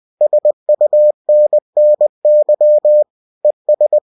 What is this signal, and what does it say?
Morse code, heard as a keyed tone, about 20 wpm, 600 Hz. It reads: SUNNY ES